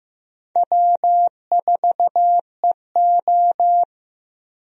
Morse code, heard as a keyed tone, about 15 wpm, 700 Hz. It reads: W4EO
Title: USA Call Signs - 4 Characters - Repeated 3x - 15wpm